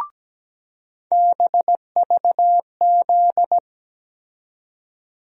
Morse code, heard as a keyed tone, about 17 wpm, 700 Hz. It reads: BVZ
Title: USA Call Sign Postfixes - 3 Letters 17wpm